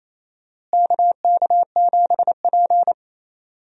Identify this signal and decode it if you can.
Morse code, heard as a keyed tone, about 28 wpm, 700 Hz. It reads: KK7P
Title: USA Call Signs - 4 Characters - Repeated 2x - 28wpm